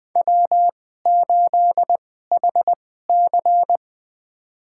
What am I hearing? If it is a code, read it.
Morse code, heard as a keyed tone, about 20 wpm, spaced as for 15 wpm, 700 Hz. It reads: W8HC
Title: USA Call Signs - 2x Character Spacing - 20wpm